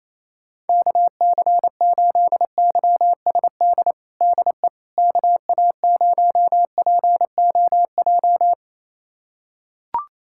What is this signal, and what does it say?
Morse code, heard as a keyed tone, about 28 wpm, 700 Hz. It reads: KC8YSD DE KA0POJ
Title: USA Callsign DE Callsign - 6 Characters - 28wpm